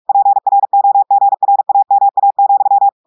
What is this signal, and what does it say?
Morse code, heard as a keyed tone, about 46 wpm, 800 Hz. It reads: PROGRAMA,
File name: audios / portugais / portugais texte en clair 2 vitesse 45 mots minutes